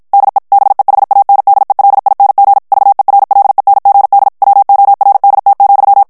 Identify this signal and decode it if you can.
Morse code, heard as a keyed tone, about 54 wpm, 800 Hz. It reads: DE BEHANDELING VERLENGD WORDT,
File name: HOLLANDAIS texte clair 2 vitesse 50 mots minute